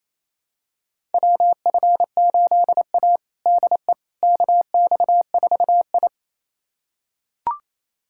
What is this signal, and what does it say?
Morse code, heard as a keyed tone, about 28 wpm, 700 Hz. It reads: WF8A DE KX4I
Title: USA Callsign DE Callsign - 4 Characters - 28wpm